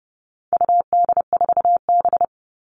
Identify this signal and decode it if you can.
Morse code, heard as a keyed tone, about 30 wpm, 700 Hz. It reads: UD4B
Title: All Cut Numbers - 4-Digit 30wpm